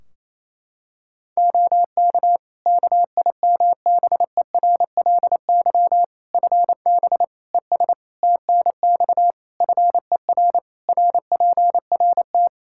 Morse code, heard as a keyed tone, about 28 wpm, 700 Hz. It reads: OK KIMBERLY FB ES TNX FER RPRT
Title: Practice QSO - Exchange 2 - Info - 28wpm